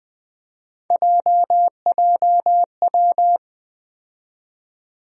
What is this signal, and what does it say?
Morse code, heard as a keyed tone, about 21 wpm, 700 Hz. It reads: JJW
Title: USA Call Sign Postfixes - 3 Letters 20wpm